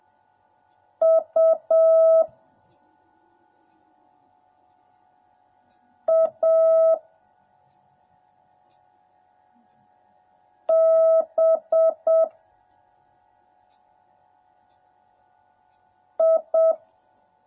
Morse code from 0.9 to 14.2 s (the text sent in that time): U A B